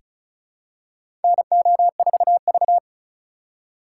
Morse code, heard as a keyed tone, about 36 wpm, 700 Hz. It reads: NO4V
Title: USA Call Signs - 4 Characters - 4x Character Spacing - 35wpm